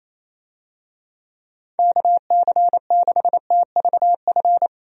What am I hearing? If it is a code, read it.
Morse code, heard as a keyed tone, about 28 wpm, 700 Hz. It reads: KC6TVF
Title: USA Call Signs - 6 Characters - Repeated 2x - 28wpm